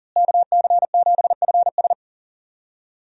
Morse code, heard as a keyed tone, about 40 wpm, 700 Hz. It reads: KC7FS
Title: USA Call Signs - 5 Characters - Repeated 3x - 40wpm